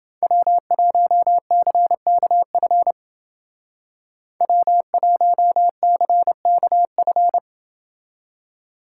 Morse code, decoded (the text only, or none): W1CKF W1CKF